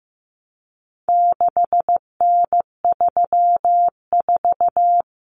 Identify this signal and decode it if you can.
Morse code, heard as a keyed tone, about 15 wpm, 700 Hz. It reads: 6N34
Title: Common Cut Numbers - 4-Digit 15wpm